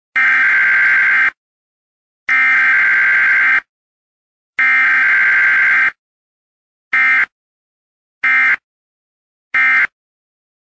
Emergency Alert System (SAME) tones (header bursts), heard as harsh buzzing bursts. It is heard from 0.1 to 9.9 s.
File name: ringtones / EAS-Same-Header